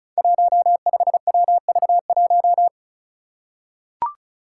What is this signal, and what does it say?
Morse code, heard as a keyed tone, about 36 wpm, 700 Hz. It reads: 15WV1
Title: Five Challenging Letter-Number Combinations 35wpm